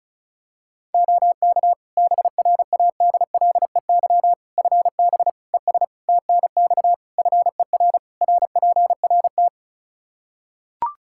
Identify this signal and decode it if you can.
Morse code, heard as a keyed tone, about 34 wpm, 700 Hz. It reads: OK BRADLEY FB ES TNX FER RPRT
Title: Practice QSO - Exchange 2 - Info - 35wpm